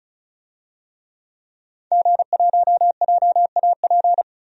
Morse code, heard as a keyed tone, about 36 wpm, 700 Hz. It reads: G1JAP
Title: DXCC-100 Call Signs - 5 Characters - 35wpm